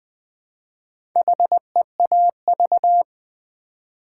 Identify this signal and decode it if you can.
Morse code, heard as a keyed tone, about 20 wpm, 700 Hz. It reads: HEAV
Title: Build A Word - 1 Letter At A Time - 5-Letter Words - 20wpm